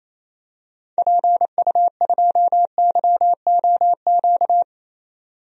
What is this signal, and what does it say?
Morse code, heard as a keyed tone, about 28 wpm, 700 Hz. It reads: PU2YOQ
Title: DXCC-100 Call Signs - 6 Characters - 28wpm